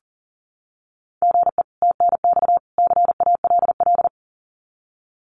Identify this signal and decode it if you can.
Morse code, heard as a keyed tone, about 40 wpm, 700 Hz. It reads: GE TNX CALL
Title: Common QSO Phrases 40wpm